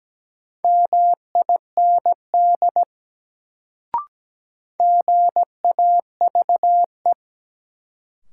Morse code, heard as a keyed tone, about 17 wpm, 700 Hz. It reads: MIND GAVE
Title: Sentences from Top 500 Words - New Word Drill - 17wpm